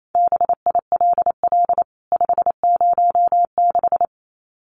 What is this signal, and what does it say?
Morse code, heard as a keyed tone, about 28 wpm, 700 Hz. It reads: BILL 506
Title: CWT Exchange 28wpm